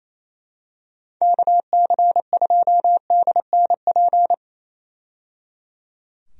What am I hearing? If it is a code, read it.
Morse code, heard as a keyed tone, about 28 wpm, 700 Hz. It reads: KC2DNP
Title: USA Call Signs - 6 Characters - 28wpm